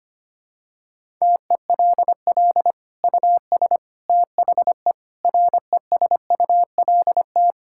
Morse code, heard as a keyed tone, about 25 wpm, 700 Hz. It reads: TELL US THE RESULT
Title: Sentences from Top 1600 Words 25wpm